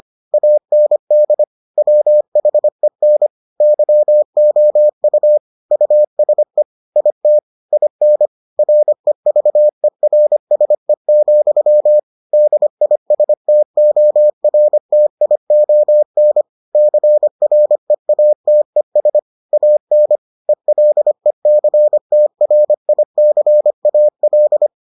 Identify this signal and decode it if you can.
Morse code, heard as a keyed tone, about 25 wpm, 600 Hz. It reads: AND WHEN YOU USE IT IN REVERSE, DISTORTION CREATES AN ELECTRICAL